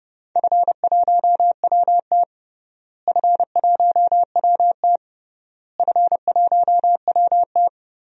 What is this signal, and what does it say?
Morse code, heard as a keyed tone, about 32 wpm, 700 Hz. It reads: F1WT F1WT F1WT
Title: DXCC-100 Call Signs - 4 Characters - Repeated 3x - 30wpm